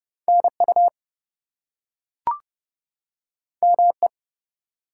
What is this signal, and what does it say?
Morse code, heard as a keyed tone, about 30 wpm, 700 Hz. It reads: NU ME
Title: US States and Canadian Provinces 30wpm